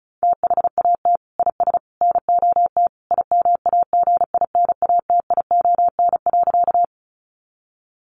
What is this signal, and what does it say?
Morse code, heard as a keyed tone, about 34 wpm, 700 Hz. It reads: THAT IS NOT IMAGINATION.
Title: Sentences from Top 1300 Words 35wpm